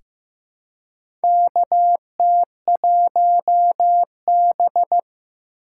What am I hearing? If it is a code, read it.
Morse code, heard as a keyed tone, about 15 wpm, 700 Hz. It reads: KT1B
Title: K1USN SST QSO Practice - 15wpm